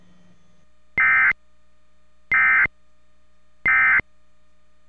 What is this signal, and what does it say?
Emergency Alert System (SAME) tones, heard as harsh buzzing bursts. It is heard from 1.0 to 4.0 s.